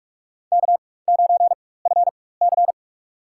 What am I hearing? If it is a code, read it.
Morse code, heard as a keyed tone, about 46 wpm, spaced as for 21 wpm, 700 Hz. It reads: K9FC
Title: USA Call Signs - 4 Characters - 4x Character Spacing - 45wpm